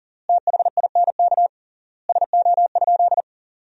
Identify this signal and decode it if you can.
Morse code, heard as a keyed tone, about 40 wpm, 700 Hz. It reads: THINK SO?
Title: 3x Word-Spacing - Sentences from Top 300 Words - 40wpm